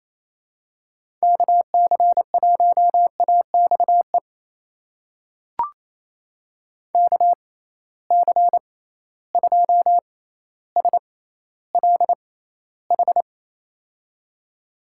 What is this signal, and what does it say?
Morse code, heard as a keyed tone, about 28 wpm, spaced as for 14 wpm, 700 Hz. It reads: KC1AXE KC2SLH
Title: USA Call Signs - 6 Characters - 6x Character Spacing - 28wpm